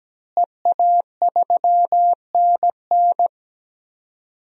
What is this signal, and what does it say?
Morse code, heard as a keyed tone, about 17 wpm, 700 Hz. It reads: EA3NN
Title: DXCC-100 Call Signs - 5 Characters - Repeated 2x - 17wpm